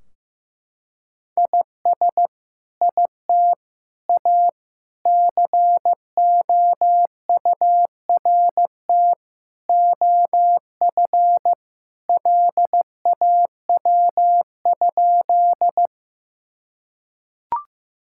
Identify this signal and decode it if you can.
Morse code, heard as a keyed tone, about 15 wpm, 700 Hz. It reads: IS IT A COURT OF LAW?
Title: Sentences from Top 1600 Words 15wpm